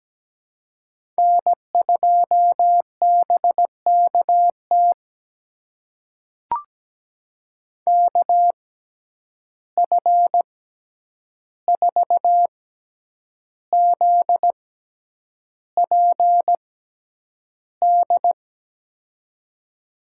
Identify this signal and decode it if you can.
Morse code, heard as a keyed tone, about 17 wpm, spaced as for 8 wpm, 700 Hz. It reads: N2BKT KF4ZPD
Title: USA Call Signs - 6x Character Spacing - 17wpm